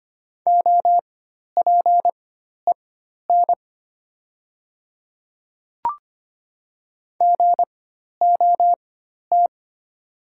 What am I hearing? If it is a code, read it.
Morse code, heard as a keyed tone, about 25 wpm, spaced as for 12 wpm, 700 Hz. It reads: OPEN GOT
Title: 4x Character Spacing - Top 300 Words - 25wpm